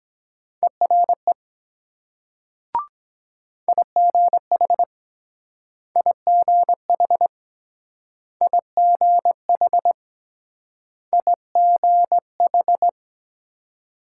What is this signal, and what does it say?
Morse code, heard as a keyed tone, about 23 wpm, 700 Hz. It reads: ERE IGH IGH IGH IGH